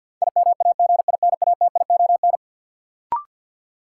Morse code, heard as a keyed tone, about 52 wpm, 700 Hz. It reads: IMAGINATION
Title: Sentences from Top 1300 Words - New Word Drill - 50wpm